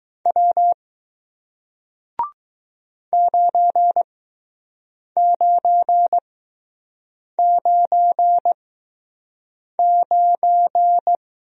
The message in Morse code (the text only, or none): W 9999